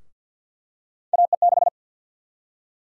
Morse code, heard as a keyed tone, about 51 wpm, 700 Hz. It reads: AE6
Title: USA Call Sign Prefixes - 50wpm